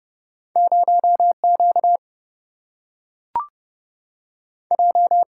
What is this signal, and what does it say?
Morse code, heard as a keyed tone, about 30 wpm, 700 Hz. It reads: 0 Q J
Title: Single-Letter Single-Number 30wpm